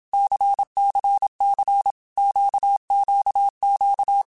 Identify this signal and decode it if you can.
Morse code, heard as a keyed tone, about 27 wpm, 800 Hz. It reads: CCC QQQ
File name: Basic Letter Groups By7Groups / LFCYQ / LFCYQ@25